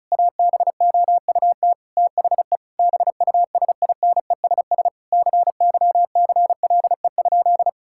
Morse code, heard as a keyed tone, about 36 wpm, 700 Hz. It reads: ABOUT THE BUSINESS CYCLE?
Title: Sentences from Top 1000 Words - Review Part 6 - 35wpm